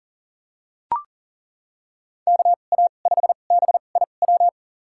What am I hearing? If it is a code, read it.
Morse code, heard as a keyed tone, about 40 wpm, spaced as for 29 wpm, 700 Hz. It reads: KA5BIW
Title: USA Call Signs - 6 Characters - 2x Character Spacing - 40wpm